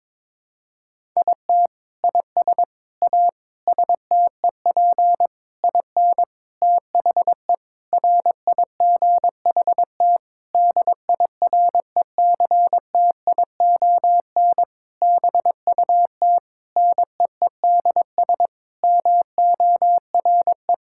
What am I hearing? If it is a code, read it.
Morse code, heard as a keyed tone, about 22 wpm, 700 Hz. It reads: IT IS A STEP IN THE RIGHT DIRECTION BUT NEEDS MORE
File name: practice / 22wpm / Sentences with 8-Word Phrases - 22wpm